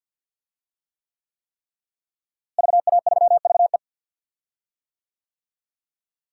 Morse code, heard as a keyed tone, about 51 wpm, 700 Hz. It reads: VA3VE
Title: DXCC-100 Call Signs - 5 Characters - Repeated 3x - 50wpm